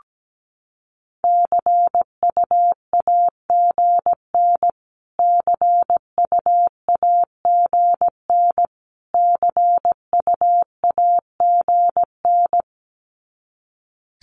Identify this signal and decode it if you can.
Morse code, heard as a keyed tone, about 17 wpm, 700 Hz. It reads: CUAGN CUAGN CUAGN
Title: Abbreviations Repeated 3x - 17wpm